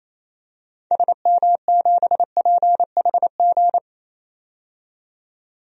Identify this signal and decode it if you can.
Morse code, heard as a keyed tone, about 28 wpm, 700 Hz. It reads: SM7PHG
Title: DXCC-100 Call Signs - 6 Characters - 28wpm